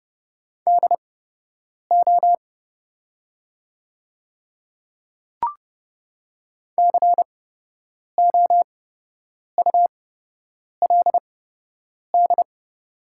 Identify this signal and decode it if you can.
Morse code, heard as a keyed tone, about 32 wpm, spaced as for 8 wpm, 700 Hz. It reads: DO COULD
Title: 8x Character Spacing - Top 200 Words - 30wpm